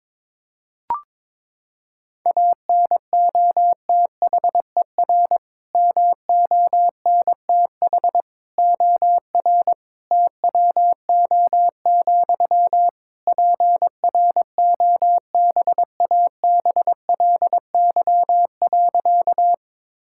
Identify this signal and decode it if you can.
Morse code, heard as a keyed tone, about 22 wpm, 700 Hz. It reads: ANOTHER MONTH OR TWO, PROBABLY.
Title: Sentences from Top 1000 Words - Review Part 5 - 22wpm